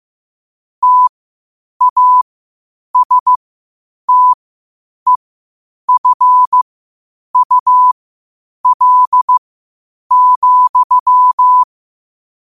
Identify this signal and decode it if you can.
Morse code, heard as a keyed tone, about 15 wpm, spaced as for 8 wpm, 1000 Hz. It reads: TASTEFUL,
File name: qotdfiles / 8wpm